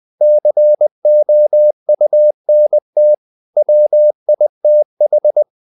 Morse code, heard as a keyed tone, about 20 wpm, 600 Hz. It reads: COUNT WITH
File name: CW / Practice / If20